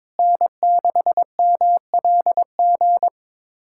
Morse code, heard as a keyed tone, about 22 wpm, 700 Hz. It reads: N6MLG
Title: USA Callsign DE Callsign - 5 Characters - 22wpm